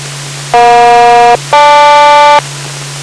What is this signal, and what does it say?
Aircraft SELCAL, heard as two pairs of tones. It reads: EJ-GL